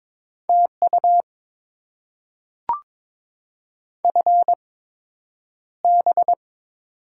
Morse code, heard as a keyed tone, about 22 wpm, spaced as for 9 wpm, 700 Hz. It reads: TU FB
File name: Top 20 QSO Elements - 8x Character Spacing - 22wpm